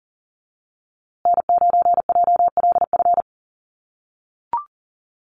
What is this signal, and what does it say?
Morse code, heard as a keyed tone, about 40 wpm, 700 Hz. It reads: N9JLF
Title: USA Call Signs Repeated 2x 40wpm